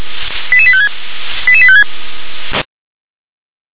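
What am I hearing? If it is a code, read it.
Five-tone selective call (ZVEI), heard as quick runs of five tones, tones of about 70 ms. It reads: 88856 88856
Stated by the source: ZVEI five-tone selective call 88151 received on an SDR.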